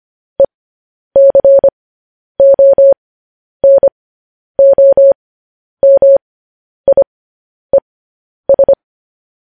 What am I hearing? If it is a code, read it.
Morse code, heard as a keyed tone, about 25 wpm, spaced as for 10 wpm, 550 Hz. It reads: ECONOMIES
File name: podcast / News headlines-1x-25x10